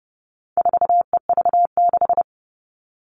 Morse code, heard as a keyed tone, about 30 wpm, 700 Hz. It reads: 4EV6
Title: All Cut Numbers - 4-Digit 30wpm